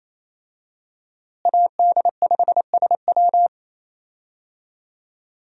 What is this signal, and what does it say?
Morse code, heard as a keyed tone, about 28 wpm, 700 Hz. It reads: AD5SW
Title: USA Call Signs - 5 Characters - Repeated 3x - 28wpm